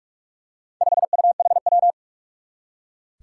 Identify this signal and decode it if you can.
Morse code, heard as a keyed tone, about 46 wpm, 700 Hz. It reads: 5UHW
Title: Four Challenging Letter-Number Combinations 45wpm